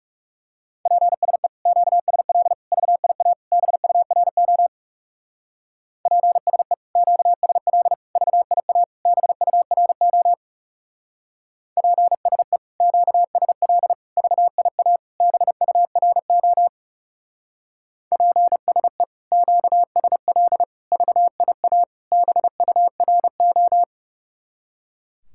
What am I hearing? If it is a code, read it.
Morse code, heard as a keyed tone, about 37 wpm, 700 Hz. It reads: PSE QSL VIA BURO PSE QSL VIA BURO PSE QSL VIA BURO PSE QSL VIA BURO
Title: Speed-Racing Common QSO Phrases - 30wpm